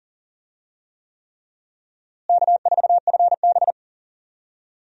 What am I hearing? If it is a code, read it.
Morse code, heard as a keyed tone, about 40 wpm, 700 Hz. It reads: K4FB
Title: USA Call Signs - 4 Characters - Repeated 3x - 40wpm